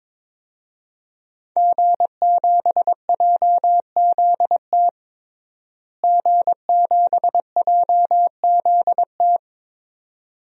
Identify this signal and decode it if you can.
Morse code, heard as a keyed tone, about 22 wpm, 700 Hz. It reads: G7JZT G7JZT
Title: DXCC-100 Call Signs - 5 Characters - Repeated 2x - 22wpm